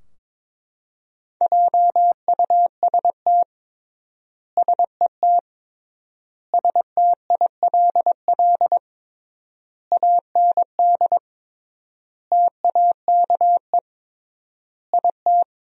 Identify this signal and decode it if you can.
Morse code, heard as a keyed tone, about 22 wpm, 700 Hz. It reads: JUST SET STILL AND TAKE IT
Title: 3x Word-Spacing - Sentences from Top 300 Words - 22wpm